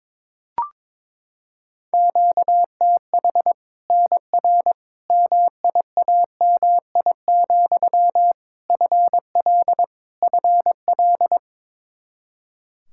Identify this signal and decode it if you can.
Morse code, heard as a keyed tone, about 22 wpm, 700 Hz. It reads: QTH NR MIAMI, FL FL
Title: Practice QSO - Exchange 1 - Beginning - 22wpm